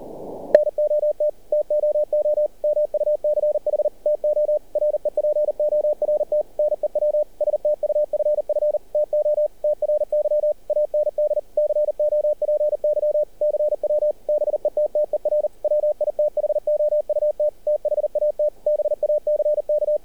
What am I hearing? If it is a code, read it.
Morse code, heard as a keyed tone, about 38 wpm, 600 Hz. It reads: NOT TOO MUCH TO REPORT NEW STUFF TO TRY AND COPY CW BETTER WITHOUT THAT BACK
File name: archived / CRSnet